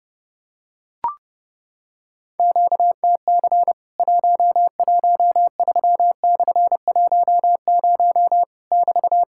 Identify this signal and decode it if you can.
Morse code, heard as a keyed tone, about 30 wpm, 700 Hz. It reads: QTC 113/10 <BT>
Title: Worked All Europe DX Contest - QTC Exchange - 30wpm